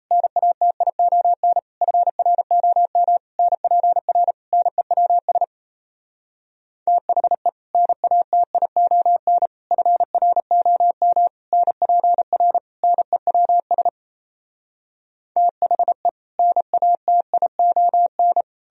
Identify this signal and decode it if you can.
Morse code, heard as a keyed tone, about 37 wpm, 700 Hz. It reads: NATION FROM NPR NEWS THE NATION FROM NPR NEWS THE NATION